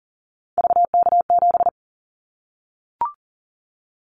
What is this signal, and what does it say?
Morse code, heard as a keyed tone, about 40 wpm, 700 Hz. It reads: VK7
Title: Callsign Prefixes for Top 100 DXCC - 40wpm